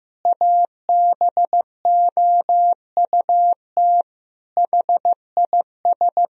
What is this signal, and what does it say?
Morse code, heard as a keyed tone, about 15 wpm, 700 Hz. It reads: ABOUT HIS
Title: Sentences from Top 1000 Words - Review Part 3 - 15wpm